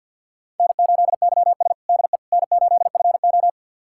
Morse code, heard as a keyed tone, about 47 wpm, 700 Hz. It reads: N8YS DE N8FO
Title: USA Callsign DE Callsign - 4 Characters - 50wpm